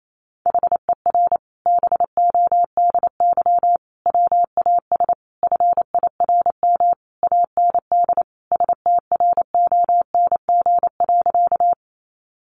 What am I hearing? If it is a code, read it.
Morse code, heard as a keyed tone, about 28 wpm, 700 Hz. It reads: HER BODY WAS FIRM AND STRONG.
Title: Sentences from Top 1400 Words 28wpm